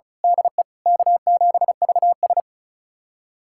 Morse code, heard as a keyed tone, about 36 wpm, 700 Hz. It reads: DE K7VS
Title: USA Callsign DE Callsign - 4 Characters - 35wpm